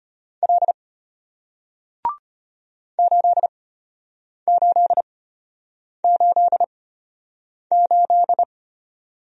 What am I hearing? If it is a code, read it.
Morse code, heard as a keyed tone, about 34 wpm, spaced as for 8 wpm, 700 Hz. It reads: L 8888